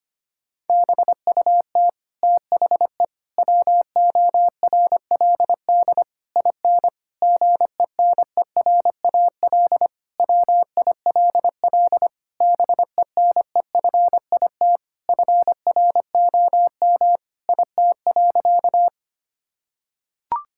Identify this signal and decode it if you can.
Morse code, heard as a keyed tone, about 25 wpm, 700 Hz. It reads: BUT THE WORLD IN GENERAL WILL BENEFIT FROM IT.
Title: Sentences from Top 1000 Words - Review Part 6 - 25wpm